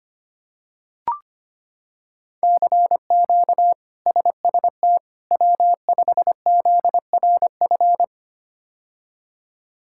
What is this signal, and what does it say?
Morse code, heard as a keyed tone, about 25 wpm, 700 Hz. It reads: CQ SST W5ZRF
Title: K1USN SST QSO Practice - 25wpm